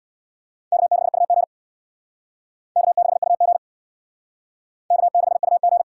Morse code, heard as a keyed tone, about 63 wpm, 700 Hz. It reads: K6UG K6UG K6UG